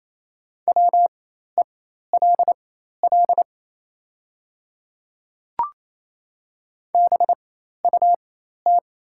Morse code, heard as a keyed tone, about 28 wpm, spaced as for 13 wpm, 700 Hz. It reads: WELL BUT
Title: 4x Character Spacing - Top 200 Words - 28wpm